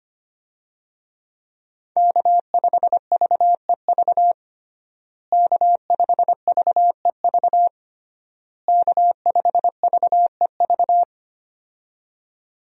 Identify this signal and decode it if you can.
Morse code, heard as a keyed tone, about 25 wpm, 700 Hz. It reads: K5VEV K5VEV K5VEV
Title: USA Call Signs - 5 Characters - Repeated 3x - 25wpm